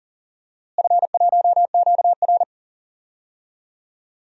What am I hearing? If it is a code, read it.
Morse code, heard as a keyed tone, about 40 wpm, 700 Hz. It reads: F1QR
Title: DXCC-100 Call Signs - 4 Characters - Repeated 2x - 40wpm